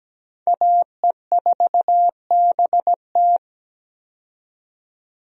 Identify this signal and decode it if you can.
Morse code, heard as a keyed tone, about 17 wpm, 700 Hz. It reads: AE4BT
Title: USA Call Signs - 5 Characters - 17wpm